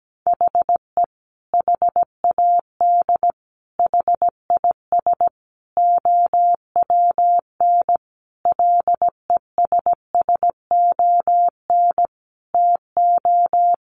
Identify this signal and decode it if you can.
Morse code, heard as a keyed tone, about 17 wpm, 700 Hz. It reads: HE HAD HIS OWN LESSON TO
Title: Sentences from Top 1200 Words 17wpm